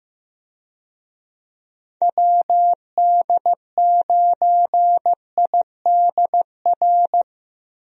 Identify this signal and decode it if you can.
Morse code, heard as a keyed tone, about 15 wpm, 700 Hz. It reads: WD9IDR
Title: USA Call Signs - 6 Characters - Repeated 2x - 15wpm